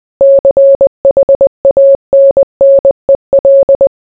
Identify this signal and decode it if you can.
Morse code, heard as a keyed tone, about 20 wpm, 550 Hz. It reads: CHANNEL